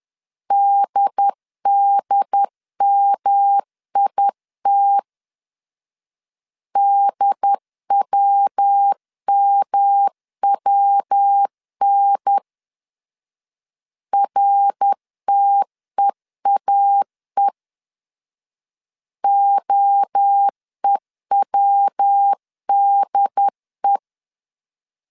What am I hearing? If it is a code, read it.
Morse code, heard as a keyed tone, about 10 wpm, 800 Hz. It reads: DDMIT DWMWN RTEAE OEWDE
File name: CW / DET / 1 3 1500C S C 10wpm 3335